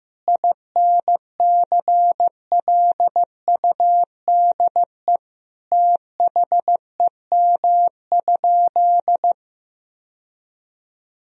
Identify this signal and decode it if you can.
Morse code, heard as a keyed tone, about 15 wpm, 700 Hz. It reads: INCLUDE THEM?
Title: Sentences from Top 1000 Words - Review Part 3 - 15wpm